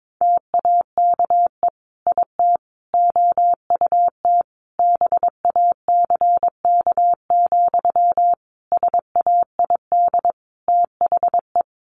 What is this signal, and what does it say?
Morse code, heard as a keyed tone, about 22 wpm, 700 Hz. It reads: TAKE IT OUT BACK, SAID THE
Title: Sentences from Top 1500 Words 22wpm